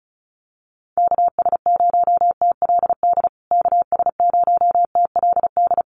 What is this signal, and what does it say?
Morse code, heard as a keyed tone, about 34 wpm, 700 Hz. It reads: KS0TLD KS0TLD
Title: USA Call Signs Repeated 3x 35wpm